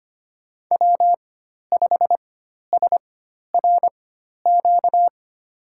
Morse code, heard as a keyed tone, about 25 wpm, spaced as for 12 wpm, 700 Hz. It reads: W5SRQ
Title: USA Call Signs - 5 Characters - 4x Character Spacing - 25wpm